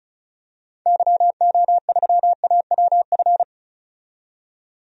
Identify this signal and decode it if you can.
Morse code, heard as a keyed tone, about 34 wpm, 700 Hz. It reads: YO3AWF
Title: DXCC-100 Call Signs - 6 Characters - 35wpm